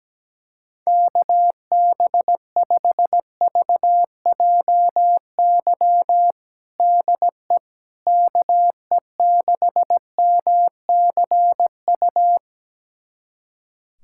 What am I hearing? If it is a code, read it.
Morse code, heard as a keyed tone, about 17 wpm, 700 Hz. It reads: KB5VJY DE KE6MCU
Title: USA Callsign DE Callsign 17wpm